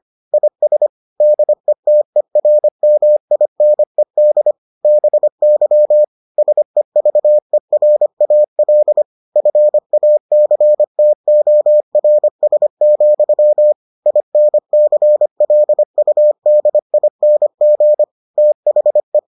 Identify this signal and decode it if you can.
Morse code, heard as a keyed tone, about 25 wpm, 600 Hz. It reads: IS DETERMINED BY SEVERAL FACTORS, INCLUDING THE